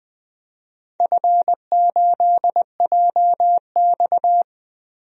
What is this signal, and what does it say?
Morse code, heard as a keyed tone, about 20 wpm, 700 Hz. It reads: F8JX
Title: DXCC-100 Call Signs - 4 Characters - 20wpm